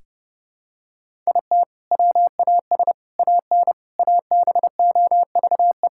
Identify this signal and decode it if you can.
Morse code, heard as a keyed tone, about 30 wpm, 700 Hz. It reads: IT WAS AN ABOVE
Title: Sentences from Top 1000 Words - Review Part 6 - 30wpm